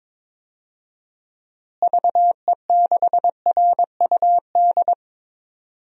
Morse code, heard as a keyed tone, about 22 wpm, 700 Hz. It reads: VE6RUD
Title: DXCC-100 Call Signs - 6 Characters - Repeated 3x - 22wpm